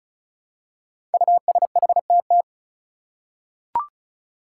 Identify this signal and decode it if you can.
Morse code, heard as a keyed tone, about 36 wpm, 700 Hz. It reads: USHTT
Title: Five Challenging Letter-Number Combinations 35wpm